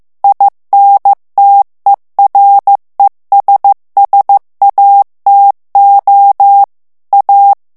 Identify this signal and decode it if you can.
Morse code, heard as a keyed tone, about 15 wpm, 800 Hz. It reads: INTERESSATO A
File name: morse mp3 / ITALIEN texte clair 2 vitesse 14 mots minute